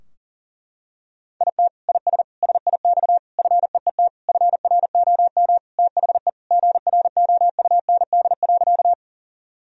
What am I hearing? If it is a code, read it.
Morse code, heard as a keyed tone, about 37 wpm, 700 Hz. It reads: IT IS SIX FEET FROM THE GROUND.
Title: Sentences from Top 1000 Words - Review Part 2 - 40wpm